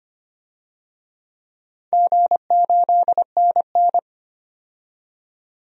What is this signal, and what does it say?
Morse code, heard as a keyed tone, about 25 wpm, 700 Hz. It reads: G8NN
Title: DXCC-100 Call Signs - 4 Characters - 25wpm